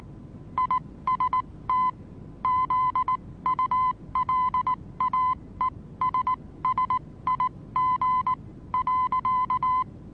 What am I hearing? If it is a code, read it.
Morse code, heard as a keyed tone, about 19 wpm, spaced as for 16 wpm, 1020 Hz. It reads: IST ZULAESSIG.
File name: repos / Morsekurs / DJ4UF Morsen26